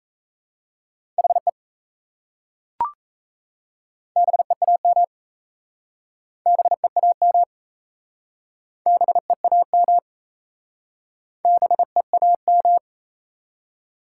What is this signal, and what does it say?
Morse code, heard as a keyed tone, about 38 wpm, 700 Hz. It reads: HE BEAM BEAM BEAM BEAM